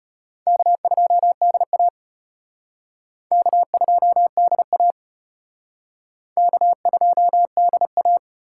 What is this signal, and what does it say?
Morse code, heard as a keyed tone, about 31 wpm, 700 Hz. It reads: K2DA K2DA K2DA